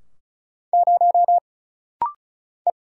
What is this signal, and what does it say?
Morse code, heard as a keyed tone, about 34 wpm, 700 Hz. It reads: 0 E
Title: Single Letter-Number - Rapid-Fire 35wpm